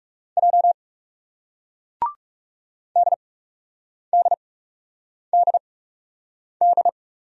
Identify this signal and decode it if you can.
Morse code, heard as a keyed tone, about 40 wpm, spaced as for 8 wpm, 700 Hz. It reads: J DDDD